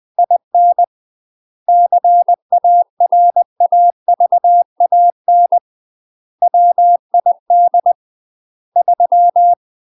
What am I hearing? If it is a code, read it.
Morse code, heard as a keyed tone, about 20 wpm, 700 Hz. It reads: IN CARAVAN WID 3
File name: UK MASTER QSOs TRAINING / 20WPM / UK MASTER QSOs 0050